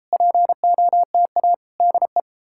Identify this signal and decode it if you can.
Morse code, heard as a keyed tone, about 38 wpm, 700 Hz. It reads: POTA DE